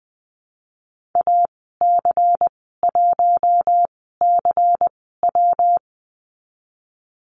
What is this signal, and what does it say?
Morse code, heard as a keyed tone, about 20 wpm, spaced as for 15 wpm, 700 Hz. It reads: AC1CW
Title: 128 Lesson 21g - Number 6 - Call Signs 20wpm